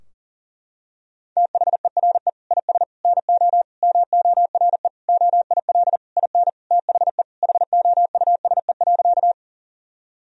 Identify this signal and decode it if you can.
Morse code, heard as a keyed tone, about 40 wpm, 700 Hz. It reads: THERE IS NO MORE OIL IN THE HOUSE.
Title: Sentences from Top 1000 Words - Review Part 4 - 40wpm